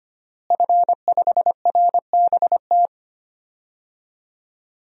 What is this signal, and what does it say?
Morse code, heard as a keyed tone, about 25 wpm, 700 Hz. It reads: F5RBT
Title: DXCC-100 Call Signs - 5 Characters - Repeated 3x - 25wpm